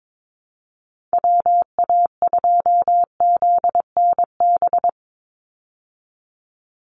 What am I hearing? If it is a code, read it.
Morse code, heard as a keyed tone, about 22 wpm, 700 Hz. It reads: WA2ZNB
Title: USA Call Signs 22wpm